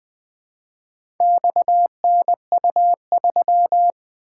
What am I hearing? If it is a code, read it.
Morse code, heard as a keyed tone, about 20 wpm, 700 Hz. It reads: XNU3
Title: Four Letter-Number Combinations - 20wpm